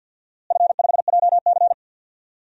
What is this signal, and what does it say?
Morse code, heard as a keyed tone, about 50 wpm, 700 Hz. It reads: F5JC
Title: DXCC-100 Call Signs - 4 Characters - Repeated 3x - 50wpm